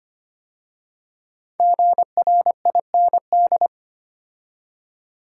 Morse code, heard as a keyed tone, about 25 wpm, 700 Hz. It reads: GRIND
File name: 5 Letter Words - First 3 Letters Given - 25wpm